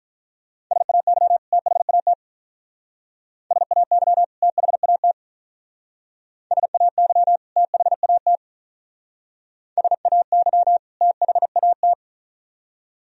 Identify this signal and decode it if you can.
Morse code, heard as a keyed tone, about 44 wpm, 700 Hz. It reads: SAY THAT SAY THAT SAY THAT SAY THAT